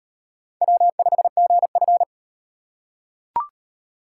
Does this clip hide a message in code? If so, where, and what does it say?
Yes: Morse code from 0.6 to 2.0 s, heard as a keyed tone, about 38 wpm, 700 Hz. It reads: W5GF